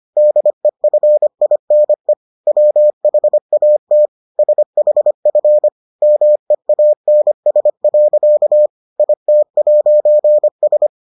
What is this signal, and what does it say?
Morse code, heard as a keyed tone, about 25 wpm, 600 Hz. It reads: DEFINE WHAT SHF MEANS. IT'S